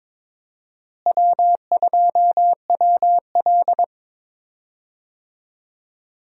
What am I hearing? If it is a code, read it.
Morse code, heard as a keyed tone, about 22 wpm, 700 Hz. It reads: W2WL
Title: USA Call Signs - 4 Characters - Repeated 3x - 22wpm